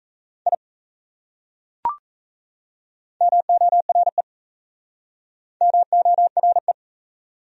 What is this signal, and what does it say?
Morse code, heard as a keyed tone, about 34 wpm, 700 Hz. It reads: I MORE MORE
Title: Speed-Racing Top 100 Words 28wpm